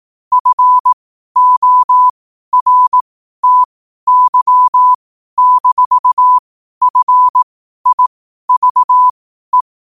Morse code, heard as a keyed tone, about 18 wpm, spaced as for 12 wpm, 1000 Hz. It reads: FORTY-FIVE
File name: qotdfiles / 12wpm